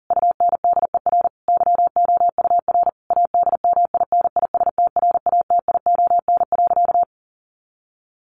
Morse code, heard as a keyed tone, about 37 wpm, 700 Hz. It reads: UNDER YOUR ADMINISTRATION.
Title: Sentences from Top 1500 Words 40wpm